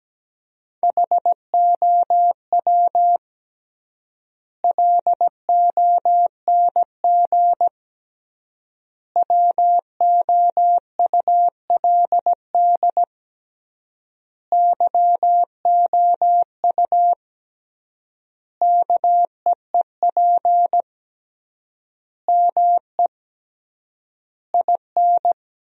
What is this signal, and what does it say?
Morse code, heard as a keyed tone, about 17 wpm, 700 Hz. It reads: HOW LONG WOULD YOU KEEP ME IN
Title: 3x Word-Spacing - Sentences from Top 500 Words - 17wpm